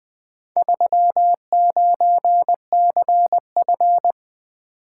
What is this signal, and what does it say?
Morse code, heard as a keyed tone, about 20 wpm, 700 Hz. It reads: 39CF
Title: Four Letter-Number Combinations - 20wpm